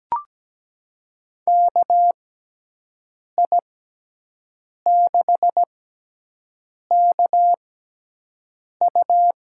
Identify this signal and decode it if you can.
Morse code, heard as a keyed tone, about 17 wpm, spaced as for 6 wpm, 700 Hz. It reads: KI6KU
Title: USA Call Signs - 6x Character Spacing - 17wpm